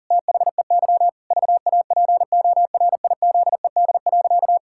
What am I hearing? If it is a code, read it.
Morse code, heard as a keyed tone, about 37 wpm, 700 Hz. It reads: THEY VAPORIZED.
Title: Pangram Challenge v2 - 40wpm